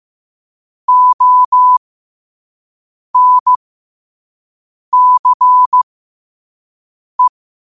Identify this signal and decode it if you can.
Morse code, heard as a keyed tone, about 15 wpm, spaced as for 5 wpm, 1000 Hz. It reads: ONCE
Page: New quotes every day in morse code at 5 Words per minute.